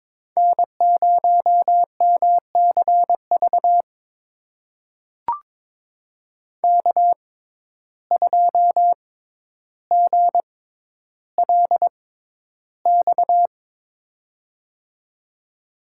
Morse code, heard as a keyed tone, about 22 wpm, spaced as for 11 wpm, 700 Hz. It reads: N0MCV K2GLX
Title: USA Call Signs - 5 Characters - 6x Character Spacing - 22wpm